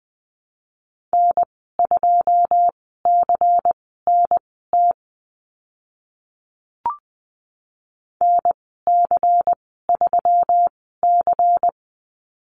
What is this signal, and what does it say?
Morse code, heard as a keyed tone, about 20 wpm, spaced as for 14 wpm, 700 Hz. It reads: N2CNT NC3C
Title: 140 Lesson 23g - Letter f - Call Signs 20wpm